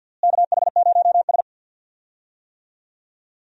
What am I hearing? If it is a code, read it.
Morse code, heard as a keyed tone, about 47 wpm, 700 Hz. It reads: KH0S
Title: USA Call Signs - 4 Characters - Repeated 3x - 50wpm